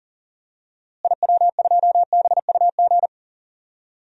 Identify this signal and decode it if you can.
Morse code, heard as a keyed tone, about 37 wpm, 700 Hz. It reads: IW2BUG
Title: DXCC-100 Call Signs - 6 Characters - Repeated 2x - 40wpm